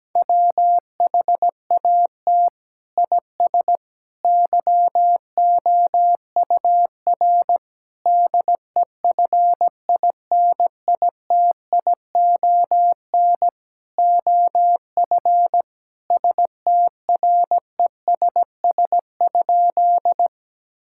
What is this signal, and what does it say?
Morse code, heard as a keyed tone, about 17 wpm, 700 Hz. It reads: WHAT IS YOUR DEFINITION OF STRESS?
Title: Sentences from Top 1000 Words - Review Part 6 - 17wpm